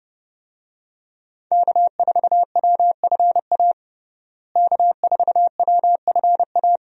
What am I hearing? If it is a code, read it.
Morse code, heard as a keyed tone, about 30 wpm, 700 Hz. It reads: K4WFA K4WFA
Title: USA Call Signs - 5 Characters - Repeated 2x - 30wpm